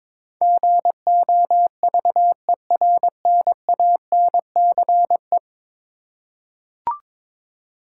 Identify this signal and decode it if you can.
Morse code, heard as a keyed tone, about 22 wpm, 700 Hz. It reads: GOVERNANCE
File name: practice / 22wpm / 10 Letter Words - First 3 Letters Given - 22wpm